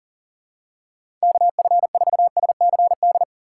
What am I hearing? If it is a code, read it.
Morse code, heard as a keyed tone, about 37 wpm, 700 Hz. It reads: KF4SCD
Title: USA Call Signs - 6 Characters - 3x Character Spacing - 40wpm